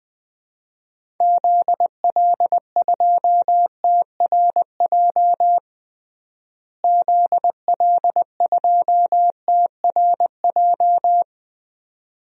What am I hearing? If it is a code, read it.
Morse code, heard as a keyed tone, about 20 wpm, 700 Hz. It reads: ZL2TRJ ZL2TRJ
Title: DXCC-100 Call Signs - 6 Characters - Repeated 2x - 20wpm